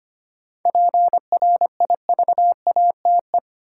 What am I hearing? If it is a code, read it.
Morse code, heard as a keyed tone, about 25 wpm, 700 Hz. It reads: PRIVATE
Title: Sentences from Top 1000 Words - New Word Drill - 25wpm